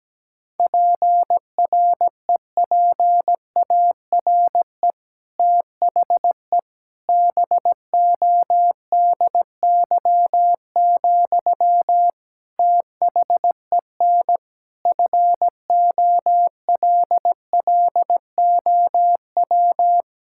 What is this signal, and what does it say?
Morse code, heard as a keyed tone, about 17 wpm, 700 Hz. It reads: PREPARE THE BODY, THEN FOLLOW
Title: Sentences from Top 1000 Words - Review Part 6 - 17wpm